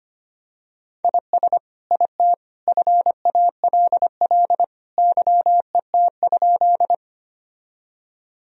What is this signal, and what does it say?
Morse code, heard as a keyed tone, about 25 wpm, 700 Hz. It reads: IS IT FALL YET?
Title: Sentences from Top 1000 Words - Review Part 2 - 25wpm